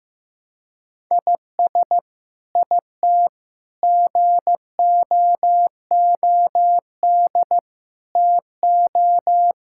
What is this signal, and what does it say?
Morse code, heard as a keyed tone, about 15 wpm, 700 Hz. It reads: IS IT GOOD TO
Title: Sentences from Top 1000 Words - Review Part 6 - 15wpm